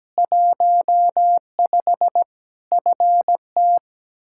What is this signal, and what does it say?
Morse code, heard as a keyed tone, about 17 wpm, 700 Hz. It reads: 15 FT
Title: K1USN SST QSO Practice 17wpm